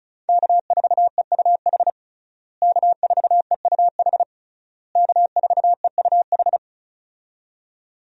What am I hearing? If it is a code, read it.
Morse code, heard as a keyed tone, about 36 wpm, 700 Hz. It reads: K4EUH K4EUH K4EUH
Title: USA Call Signs - 5 Characters - Repeated 3x - 35wpm